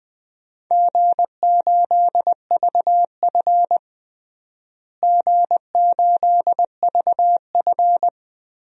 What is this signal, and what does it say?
Morse code, heard as a keyed tone, about 20 wpm, 700 Hz. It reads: G8VF G8VF
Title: DXCC-100 Call Signs - 4 Characters - Repeated 2x - 20wpm